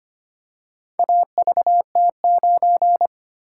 Morse code, heard as a keyed tone, about 25 wpm, 700 Hz. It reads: AVT9
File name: practice / 25wpm / Four Letter-Number Combinations - 25wpm